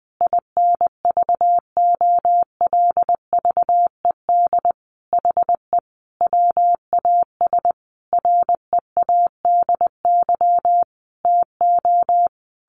Morse code, heard as a keyed tone, about 20 wpm, 700 Hz. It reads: INVOLVED HE WAS READY TO
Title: Sentences from Top 1400 Words 20wpm